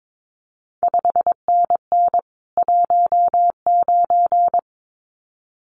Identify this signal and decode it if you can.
Morse code, heard as a keyed tone, about 22 wpm, 700 Hz. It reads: 5NN 19
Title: CQ WW DX Contest Exchange Encore - 22wpm